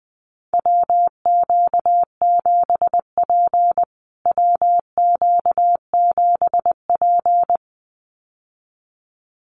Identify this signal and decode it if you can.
Morse code, heard as a keyed tone, about 19 wpm, 700 Hz. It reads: WQ7P WQ7P
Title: USA Call Signs Repeated 2x 20wpm